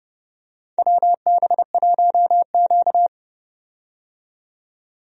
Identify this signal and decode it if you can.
Morse code, heard as a keyed tone, about 31 wpm, 700 Hz. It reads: WB1Q
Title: USA Call Signs - 4 Characters - 30wpm